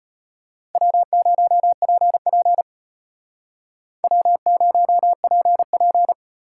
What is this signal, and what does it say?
Morse code, heard as a keyed tone, about 32 wpm, 700 Hz. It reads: W0PP W0PP